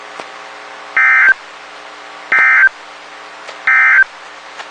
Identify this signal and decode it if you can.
Emergency Alert System (SAME) tones, heard as harsh buzzing bursts. It is heard from 0.9 to 4.1 s.
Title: WXK91 NOAA Weather Radio